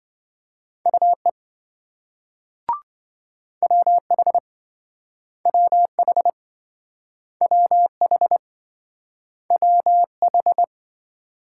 Morse code, heard as a keyed tone, about 27 wpm, 700 Hz. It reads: UE WH WH WH WH